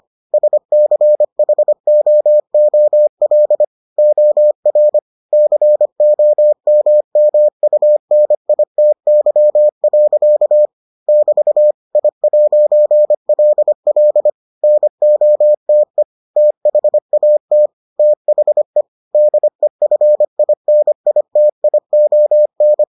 Morse code, heard as a keyed tone, about 25 wpm, 600 Hz. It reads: SCHOOL OR COMMUNITY. <BT> I'LL NOTE THAT THE DEFINITION